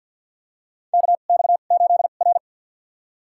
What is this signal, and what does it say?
Morse code, heard as a keyed tone, about 52 wpm, spaced as for 36 wpm, 700 Hz. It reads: KX8R
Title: USA Call Signs - 2x Character Spacing - 50wpm